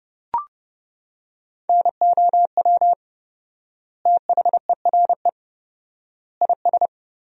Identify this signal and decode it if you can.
Morse code, heard as a keyed tone, about 32 wpm, 700 Hz. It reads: NOW THERE IS
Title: 4x Word-Spacing - Sentences from Top 100 Words - 30wpm